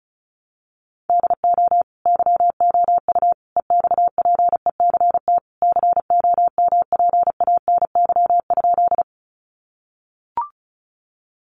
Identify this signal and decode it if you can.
Morse code, heard as a keyed tone, about 34 wpm, 700 Hz. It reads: DO YOU EXPECT COMPANY?
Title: Sentences from Top 1100 Words 35wpm